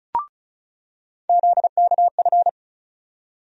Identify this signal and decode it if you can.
Morse code, heard as a keyed tone, about 36 wpm, 700 Hz. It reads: ZKF
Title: USA Call Sign Postfixes - 3 Letters 35wpm